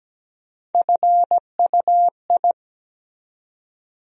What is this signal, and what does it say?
Morse code, heard as a keyed tone, about 17 wpm, 700 Hz. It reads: FUI
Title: USA Call Sign Postfixes - 3 Letters 17wpm